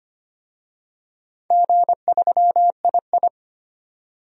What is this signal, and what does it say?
Morse code, heard as a keyed tone, about 25 wpm, 700 Hz. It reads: G3II
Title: DXCC-100 Call Signs - 4 Characters - 25wpm